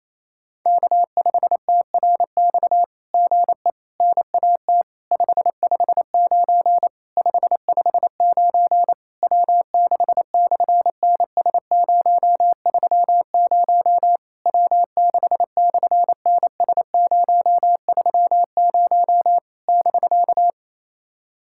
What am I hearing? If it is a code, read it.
Morse code, heard as a keyed tone, about 27 wpm, 700 Hz. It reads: K5TRX GE NAT 559 559 W6/NS030 W6/NS030 <BK>
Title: SOTA QSO Practice 28wpm